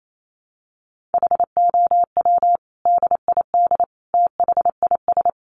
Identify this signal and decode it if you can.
Morse code, heard as a keyed tone, about 28 wpm, 700 Hz. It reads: HOW DID THIS
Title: Sentences from Top 1400 Words 28wpm